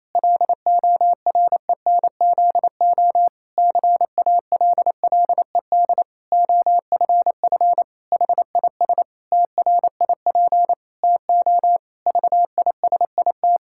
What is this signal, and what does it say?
Morse code, heard as a keyed tone, about 27 wpm, 700 Hz. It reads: LORENZO CALLED OFF HIS TRIP TO VISIT
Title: Pangram Challenge v2 - 28wpm